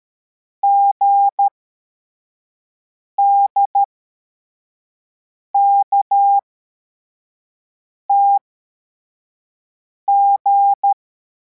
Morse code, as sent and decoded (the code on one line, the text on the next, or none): --. -.. -.- - --.
GDKTG